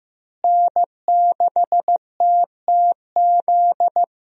Morse code, heard as a keyed tone, about 15 wpm, 700 Hz. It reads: N6TTZ
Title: USA Call Signs - 5 Characters - Repeated 3x - 15wpm